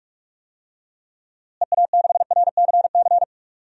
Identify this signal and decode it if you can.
Morse code, heard as a keyed tone, about 44 wpm, 700 Hz. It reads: EA6RCC
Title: DXCC-100 Call Signs - 6 Characters - Repeated 3x - 45wpm